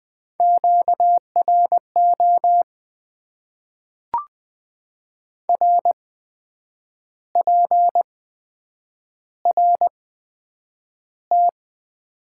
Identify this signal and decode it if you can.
Morse code, heard as a keyed tone, about 20 wpm, spaced as for 8 wpm, 700 Hz. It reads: QRO RPRT
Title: Top 90 QSO Elements - 8x Character Spacing - 20wpm